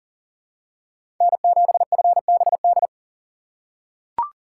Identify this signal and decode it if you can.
Morse code, heard as a keyed tone, about 40 wpm, 700 Hz. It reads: N7FBD
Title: USA Call Signs - 5 Characters - 2x Character Spacing - 40wpm